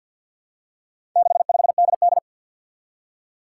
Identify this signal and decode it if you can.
Morse code, heard as a keyed tone, about 50 wpm, 700 Hz. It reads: 65DD
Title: Four Challenging Letter-Number Combinations 50wpm